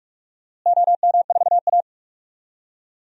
Morse code, heard as a keyed tone, about 44 wpm, 700 Hz. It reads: OM4A
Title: DXCC-100 Call Signs - 4 Characters - Repeated 2x - 45wpm